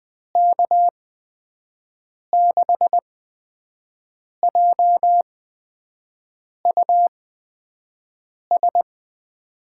Morse code, heard as a keyed tone, about 20 wpm, spaced as for 5 wpm, 700 Hz. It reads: K6JUS
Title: USA Call Signs - 6 Characters - 8x Character Spacing - 20wpm